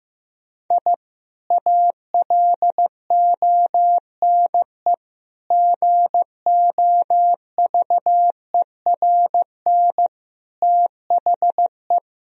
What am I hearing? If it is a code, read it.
Morse code, heard as a keyed tone, about 15 wpm, 700 Hz. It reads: I ALONE GOVERN THE
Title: Sentences from Top 1000 Words - Review Part 2 - 15wpm